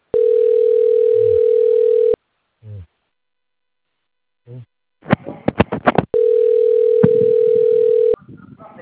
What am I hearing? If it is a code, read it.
Ringback tone (North America), 440 + 480 Hz.